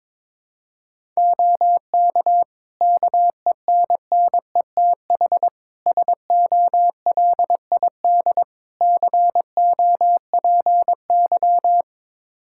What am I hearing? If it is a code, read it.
Morse code, heard as a keyed tone, about 22 wpm, 700 Hz. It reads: OK KENNETH SOLID COPY
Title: Practice QSO - Exchange 3 - More Info - 22wpm